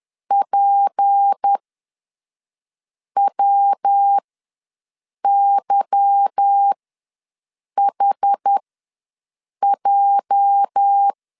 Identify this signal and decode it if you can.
Morse code, heard as a keyed tone, about 10 wpm, spaced as for 5 wpm, 800 Hz. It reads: PWYHJ
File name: LEZIONI / 8 YJ45 / 8 YJ ws 10wpm 4043